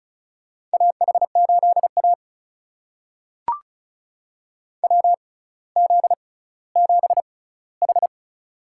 Morse code, heard as a keyed tone, about 36 wpm, spaced as for 18 wpm, 700 Hz. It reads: AH8A WZ7H
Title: USA Call Signs - 4 Characters - 6x Character Spacing - 35wpm